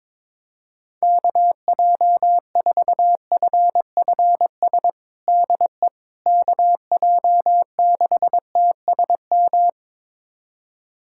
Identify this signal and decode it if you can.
Morse code, heard as a keyed tone, about 21 wpm, 700 Hz. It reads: KJ4FFS DE KJ6TSM
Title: USA Callsign DE Callsign - 6 Characters - 22wpm